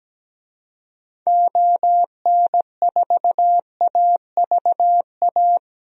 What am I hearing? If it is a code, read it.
Morse code, heard as a keyed tone, about 17 wpm, 700 Hz. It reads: ON4AVA
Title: DXCC-100 Call Signs - 6 Characters - Repeated 3x - 17wpm